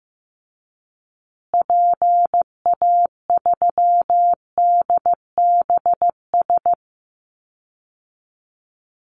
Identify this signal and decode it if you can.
Morse code, heard as a keyed tone, about 15 wpm, 700 Hz. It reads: PA3DBS
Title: Call Signs Repeated 3x - 15wpm